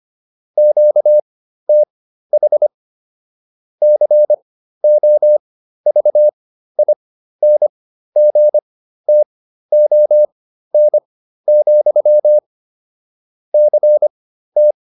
Morse code, heard as a keyed tone, about 25 wpm, spaced as for 13 wpm, 600 Hz. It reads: QTH COVINGTON, CT